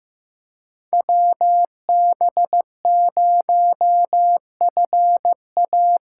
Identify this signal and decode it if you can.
Morse code, heard as a keyed tone, about 15 wpm, 700 Hz. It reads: WB0FA
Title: USA Call Signs - 6 Characters - 8x Character Spacing - 15wpm